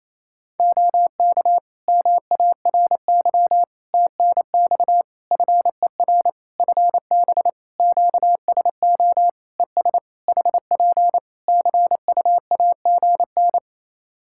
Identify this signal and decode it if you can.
Morse code, heard as a keyed tone, about 28 wpm, 700 Hz. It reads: OK MARY TNX FER FB QSO ES HP CUAGN
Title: Practice QSO - Exchange 4 - Ending - 28wpm